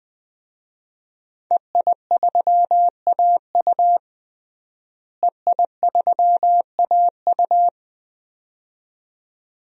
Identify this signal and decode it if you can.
Morse code, heard as a keyed tone, about 21 wpm, 700 Hz. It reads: EI3AU EI3AU
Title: DXCC-100 Call Signs - 5 Characters - Repeated 2x - 20wpm